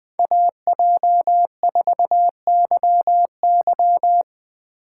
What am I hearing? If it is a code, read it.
Morse code, heard as a keyed tone, about 20 wpm, 700 Hz. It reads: AJ4YY
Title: USA Call Signs - 5 Characters - 3x Character Spacing - 20wpm